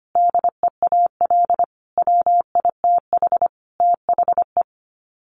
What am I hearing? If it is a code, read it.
Morse code, heard as a keyed tone, about 25 wpm, 700 Hz. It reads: DEAL WITH THE
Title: Sets of 5 Words Encore2 25wpm